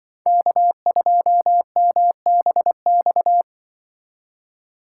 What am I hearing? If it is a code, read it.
Morse code, heard as a keyed tone, about 24 wpm, 700 Hz. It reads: K2MBX
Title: Speed Racing - USA 1x3 Call Signs 20wpm